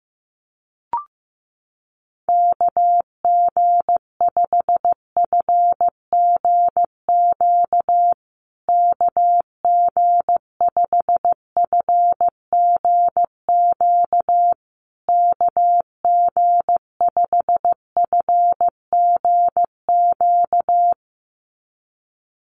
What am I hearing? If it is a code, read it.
Morse code, heard as a keyed tone, about 15 wpm, 700 Hz. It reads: KG5FGQ KG5FGQ KG5FGQ
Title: USA Call Signs Repeated 3x 15wpm